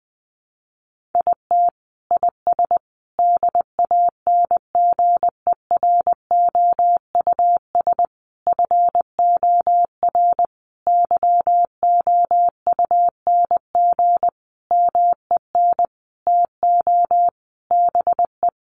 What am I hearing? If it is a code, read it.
Morse code, heard as a keyed tone, about 20 wpm, 700 Hz. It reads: IT IS DANGEROUS FOR YOUNG MEN TO BE
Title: Sentences from Top 1200 Words 20wpm